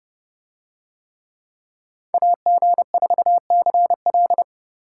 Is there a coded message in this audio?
Yes: Morse code from 2.1 to 4.4 s, heard as a keyed tone, about 31 wpm, 700 Hz. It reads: AG4CL